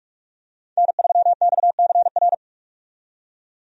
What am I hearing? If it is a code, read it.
Morse code, heard as a keyed tone, about 42 wpm, 700 Hz. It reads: N3XCR
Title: USA Call Signs - 5 Characters - 45wpm